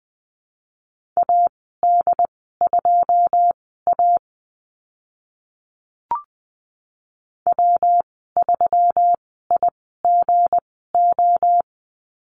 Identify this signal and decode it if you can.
Morse code, heard as a keyed tone, about 20 wpm, spaced as for 14 wpm, 700 Hz. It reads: AD2A W3IGO
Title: 161 Lesson 26g - Letter g - Call Signs 20wpm